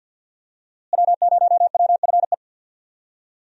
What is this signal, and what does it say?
Morse code, heard as a keyed tone, about 52 wpm, 700 Hz. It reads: W0WFE
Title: USA Call Signs - 5 Characters - Repeated 3x - 50wpm